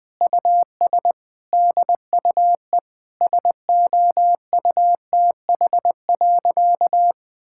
Morse code, heard as a keyed tone, about 20 wpm, 700 Hz. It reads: US DUE SOUTH.
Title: Sentences from Top 1000 Words - Review Part 6 - 20wpm